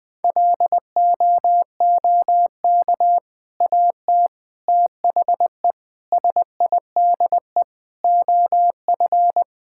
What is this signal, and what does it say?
Morse code, heard as a keyed tone, about 21 wpm, 700 Hz. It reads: LOOK AT THE SIDE OF
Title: Sentences with 3-Word Phrases - 20wpm